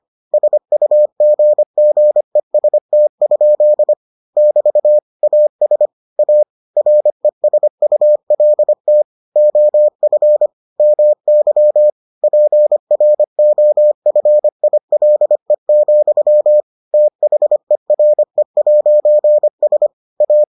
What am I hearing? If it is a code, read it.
Morse code, heard as a keyed tone, about 25 wpm, 600 Hz. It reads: SUGGEST? <BT> AS A RESULT OF MY PROFILE, THERE'S A